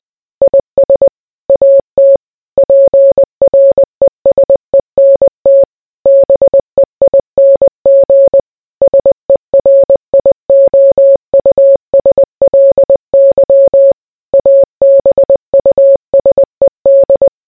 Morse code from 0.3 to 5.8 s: IS AT PRESENT